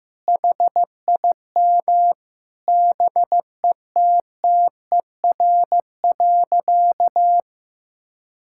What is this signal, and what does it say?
Morse code, heard as a keyed tone, about 15 wpm, 700 Hz. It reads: HIM BETTER.
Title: Sentences from Top 1600 Words 15wpm